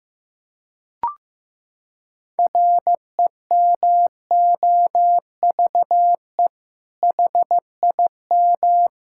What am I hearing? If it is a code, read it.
Morse code, heard as a keyed tone, about 15 wpm, 700 Hz. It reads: REMOVE HIM
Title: Sentences from Top 1000 Words - Review Part 5 - 15wpm